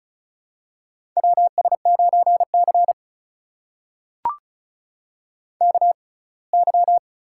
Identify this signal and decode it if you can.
Morse code, heard as a keyed tone, about 36 wpm, spaced as for 24 wpm, 700 Hz. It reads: WS9C KY
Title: USA Call Signs - 4 Characters - 6x Character Spacing - 35wpm